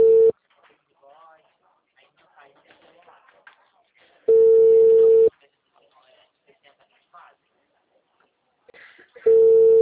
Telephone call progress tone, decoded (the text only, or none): ringback tone